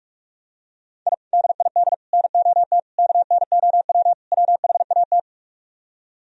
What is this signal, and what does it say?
Morse code, heard as a keyed tone, about 44 wpm, 700 Hz. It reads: I DID NOT KNOW WHAT